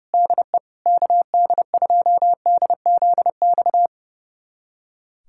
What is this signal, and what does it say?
Morse code, heard as a keyed tone, about 29 wpm, 700 Hz. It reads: DE KD2DZX
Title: USA Callsign DE Callsign - 6 Characters - 30wpm